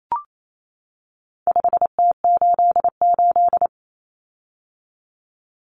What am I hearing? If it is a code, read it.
Morse code, heard as a keyed tone, about 28 wpm, 700 Hz. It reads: 5T88
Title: Common Cut Numbers - 4-Digit 28wpm